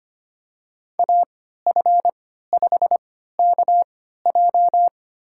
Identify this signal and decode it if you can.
Morse code, heard as a keyed tone, about 25 wpm, spaced as for 14 wpm, 700 Hz. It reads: AF5KJ
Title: USA Call Signs - 3x Character Spacing - 25wpm